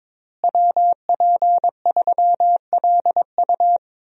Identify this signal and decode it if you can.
Morse code, heard as a keyed tone, about 22 wpm, 700 Hz. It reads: WP3LU
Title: USA Call Signs - 5 Characters - Repeated 2x - 22wpm